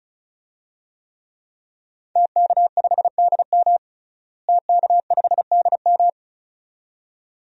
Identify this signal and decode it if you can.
Morse code, heard as a keyed tone, about 36 wpm, 700 Hz. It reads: TK5DM TK5DM
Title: DXCC-100 Call Signs - 5 Characters - Repeated 2x - 35wpm